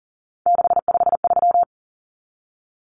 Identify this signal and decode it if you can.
Morse code, heard as a keyed tone, about 40 wpm, 700 Hz. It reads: 653
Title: Common Cut Numbers - 3-Digit 40wpm